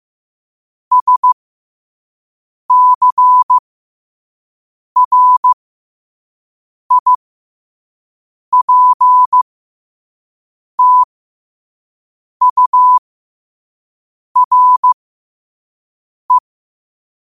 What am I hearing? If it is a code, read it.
Morse code, heard as a keyed tone, about 15 wpm, spaced as for 5 wpm, 1000 Hz. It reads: SCRIPTURE
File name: qotdfiles / 5wpm